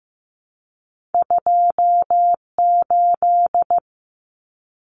Morse code, heard as a keyed tone, about 15 wpm, 700 Hz. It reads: 28
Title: Numbers - 2-Digit 15wpm